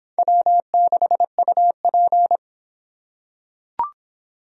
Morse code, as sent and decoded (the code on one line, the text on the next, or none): .-- -.... ..- .--.
W6UP